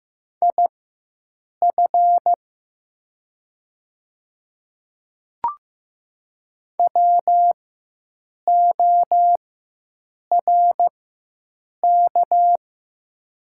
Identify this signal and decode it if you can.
Morse code, heard as a keyed tone, about 15 wpm, spaced as for 7 wpm, 700 Hz. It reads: IF WORK
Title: 4x Character Spacing - Top 200 Words - 15wpm